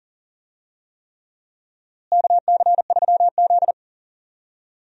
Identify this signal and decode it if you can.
Morse code, heard as a keyed tone, about 40 wpm, 700 Hz. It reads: KC3Z
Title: USA Call Signs - 4 Characters - Repeated 3x - 40wpm